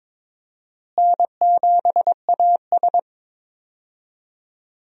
Morse code, heard as a keyed tone, about 22 wpm, 700 Hz. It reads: N7AS
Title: USA Call Signs - 4 Characters - Repeated 3x - 22wpm